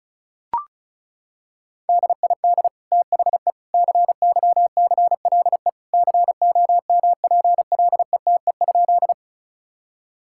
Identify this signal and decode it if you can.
Morse code, heard as a keyed tone, about 34 wpm, 690 Hz. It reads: DID THE CYCLE COMPLETE?
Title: Sentences from Top 1000 Words - Review Part 6 - 35wpm